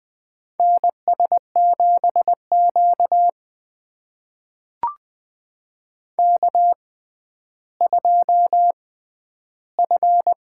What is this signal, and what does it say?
Morse code, heard as a keyed tone, about 19 wpm, spaced as for 11 wpm, 700 Hz. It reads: NS7Q K2F
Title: USA Call Signs - 4 Characters - 6x Character Spacing - 20wpm